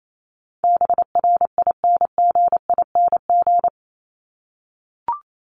Morse code, heard as a keyed tone, about 28 wpm, 700 Hz. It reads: BRINGING
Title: ING Suffix - Top 100 Words with Base Words 28wpm